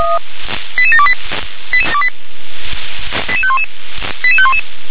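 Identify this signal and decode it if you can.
Five-tone selective call (ZVEI), heard as quick runs of five tones, tones of about 70 ms. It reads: 88728 88511 88511
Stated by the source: ZVEI five-tone call received by SDR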